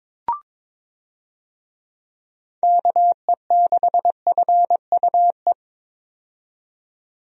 Morse code, heard as a keyed tone, about 22 wpm, 700 Hz. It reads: KE6FUE
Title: USA Call Signs - 6 Characters - 22wpm